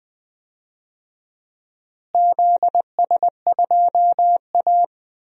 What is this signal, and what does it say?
Morse code, heard as a keyed tone, about 20 wpm, 700 Hz. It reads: ZS2A
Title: DXCC-100 Call Signs - 4 Characters - Repeated 2x - 20wpm